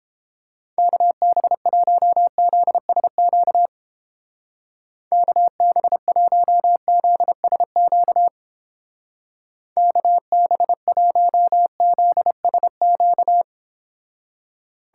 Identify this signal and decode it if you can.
Morse code, heard as a keyed tone, about 29 wpm, 700 Hz. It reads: KB1ZSQ KB1ZSQ KB1ZSQ